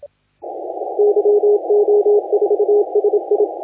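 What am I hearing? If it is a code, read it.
Morse code, heard as a keyed tone, about 27 wpm, 410 Hz. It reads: YO4SI
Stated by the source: Morse contest audio recorded by LA4C during the SAC CW 2018 contest.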